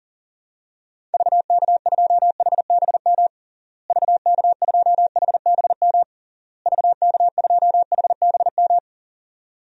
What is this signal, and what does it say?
Morse code, heard as a keyed tone, about 37 wpm, 700 Hz. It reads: VK2HBM VK2HBM VK2HBM
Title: DXCC-100 Call Signs - 6 Characters - Repeated 3x - 40wpm